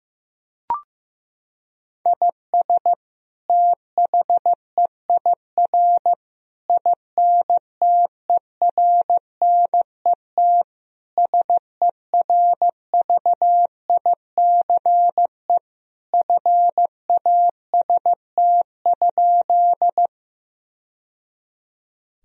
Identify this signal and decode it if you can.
Morse code, heard as a keyed tone, about 15 wpm, 700 Hz. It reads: IS THEIR INTERNET SERVICE FAST?
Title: Sentences from Top 1000 Words - Review Part 4 - 15wpm